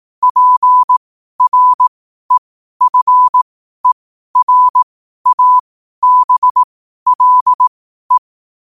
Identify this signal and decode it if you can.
Morse code, heard as a keyed tone, about 18 wpm, spaced as for 12 wpm, 1000 Hz. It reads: PREFERABLE
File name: 12wpm